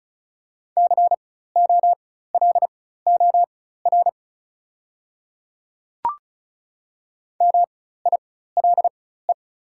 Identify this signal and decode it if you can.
Morse code, heard as a keyed tone, about 36 wpm, spaced as for 16 wpm, 690 Hz. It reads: COLOR MILE
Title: 4x Character Spacing - Top 300 Words - 35wpm